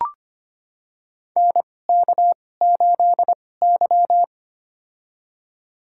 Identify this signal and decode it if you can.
Morse code, heard as a keyed tone, about 25 wpm, spaced as for 18 wpm, 700 Hz. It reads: NK8Y
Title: USA Call Signs - 4 Characters - 2x Character Spacing - 25wpm